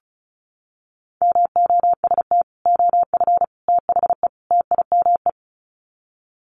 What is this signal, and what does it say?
Morse code, heard as a keyed tone, about 34 wpm, 700 Hz. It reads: MOST OF THE TIME
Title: Sets of 4 Words 35wpm